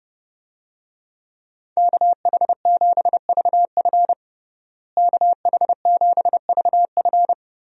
Morse code, heard as a keyed tone, about 31 wpm, 700 Hz. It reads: KH7VF KH7VF
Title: USA Call Signs - 5 Characters - Repeated 2x - 30wpm